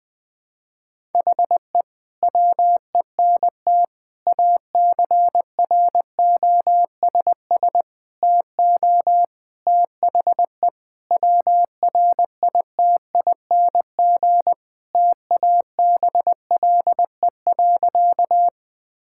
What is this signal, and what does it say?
Morse code, heard as a keyed tone, about 20 wpm, 700 Hz. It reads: HE WENT ACROSS TO THE WRITING TABLE.
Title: Sentences from Top 1000 Words - Review Part 5 - 20wpm